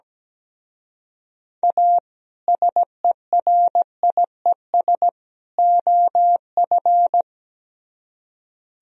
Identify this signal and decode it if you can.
Morse code, heard as a keyed tone, about 18 wpm, 700 Hz. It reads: A SERIES OF
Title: Speed-Racing - Sets of 3 Words 17wpm